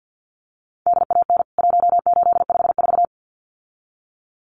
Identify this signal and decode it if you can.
Morse code, heard as a keyed tone, about 48 wpm, 700 Hz. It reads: DAN 1854
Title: CWT Exchange 50wpm